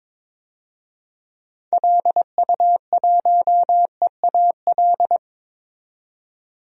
Morse code, heard as a keyed tone, about 22 wpm, 700 Hz. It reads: LU1EAL
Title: DXCC-100 Call Signs - 6 Characters - 22wpm